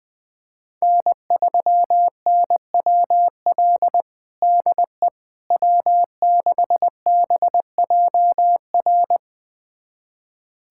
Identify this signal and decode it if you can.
Morse code, heard as a keyed tone, about 20 wpm, 700 Hz. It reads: N3NWL DE W6BJR
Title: USA Callsign DE Callsign - 5 Characters - 20wpm